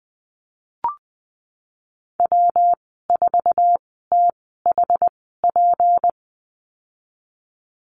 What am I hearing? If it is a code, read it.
Morse code, heard as a keyed tone, about 20 wpm, spaced as for 15 wpm, 700 Hz. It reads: W4THP
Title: 186 Lesson 30g - Letter b - Call Signs 20wpm